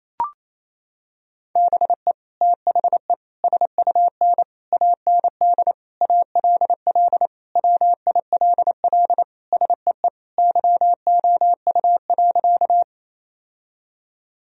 Morse code, heard as a keyed tone, about 28 wpm, 700 Hz. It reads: BE THE SUN AND ALL WILL SEE YOU.
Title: Sentences from Top 1000 Words - Review Part 1 - 28wpm